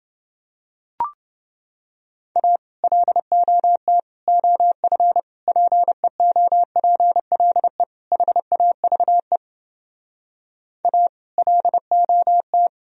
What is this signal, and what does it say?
Morse code, heard as a keyed tone, about 31 wpm, 700 Hz. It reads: A LOT OF PEOPLE HAVE A LOT